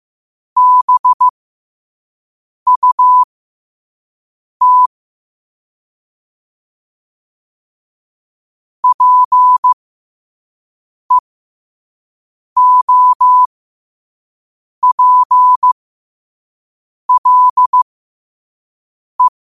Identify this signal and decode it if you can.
Morse code, heard as a keyed tone, about 15 wpm, spaced as for 5 wpm, 1000 Hz. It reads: BUT PEOPLE
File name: qotdfiles / 5wpm